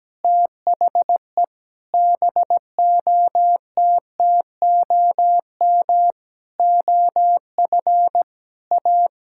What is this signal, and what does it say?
Morse code, heard as a keyed tone, about 17 wpm, 700 Hz. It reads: THE BOTTOM OF A
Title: Sentences from Top 1000 Words - Review Part 6 - 17wpm